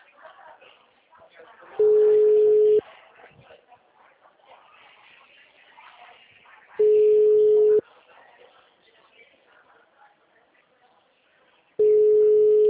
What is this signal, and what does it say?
Ringback tone (Europe, 425 Hz), with speech.
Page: Araras (FGTS)